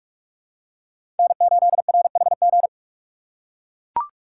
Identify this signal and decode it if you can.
Morse code, heard as a keyed tone, about 41 wpm, 700 Hz. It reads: N8RHG
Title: USA Call Signs - 5 Characters - 3x Character Spacing - 45wpm